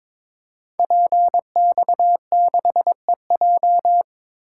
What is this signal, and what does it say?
Morse code, heard as a keyed tone, about 23 wpm, 700 Hz. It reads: PX6EJ
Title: Code Groups - 5 Letter:Numbers - 22wpm